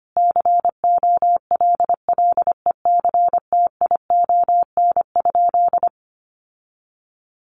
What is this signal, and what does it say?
Morse code, heard as a keyed tone, about 25 wpm, 700 Hz. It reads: COLLECTION?
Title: Sentences from Top 1200 Words 25wpm